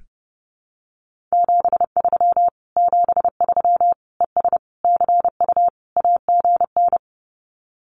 Morse code, heard as a keyed tone, about 30 wpm, 700 Hz. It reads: 73 73 ES CU AGN
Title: Common QSO Phrases Encore 30wpm